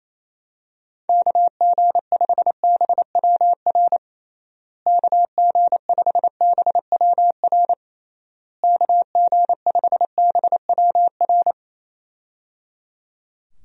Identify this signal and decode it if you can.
Morse code, heard as a keyed tone, about 28 wpm, 700 Hz. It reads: KG5BWR KG5BWR KG5BWR
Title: USA Call Signs - 6 Characters - Repeated 3x - 28wpm